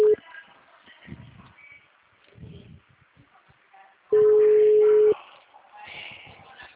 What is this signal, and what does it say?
Ringback tone (Europe, 425 Hz), with speech.